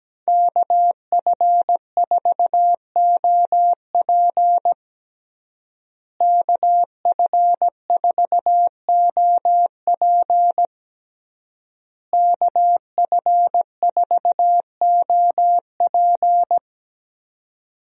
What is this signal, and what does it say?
Morse code, heard as a keyed tone, about 17 wpm, 700 Hz. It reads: KF4OP KF4OP KF4OP
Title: USA Call Signs - 5 Characters - Repeated 3x - 17wpm